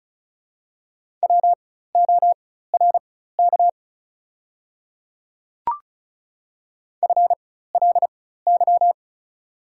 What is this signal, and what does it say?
Morse code, heard as a keyed tone, about 36 wpm, spaced as for 16 wpm, 690 Hz. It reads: WORK FLY
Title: 4x Character Spacing - Top 500 Words - 35wpm